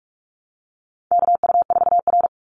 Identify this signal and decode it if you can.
Morse code, heard as a keyed tone, about 44 wpm, 700 Hz. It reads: KU4R
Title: NA SOTA Chaser Callsigns 45wpm